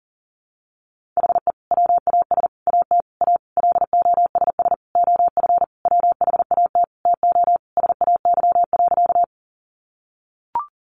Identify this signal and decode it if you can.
Morse code, heard as a keyed tone, about 40 wpm, 700 Hz. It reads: HE WAS AT A LOSS OF WHAT TO SAY.
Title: Sentences from Top 1400 Words 40wpm